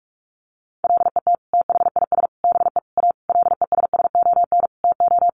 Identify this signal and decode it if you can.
Morse code, heard as a keyed tone, about 44 wpm, 700 Hz. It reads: LET THIS BE A LESSON TO
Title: Sentences from Top 1200 Words 45wpm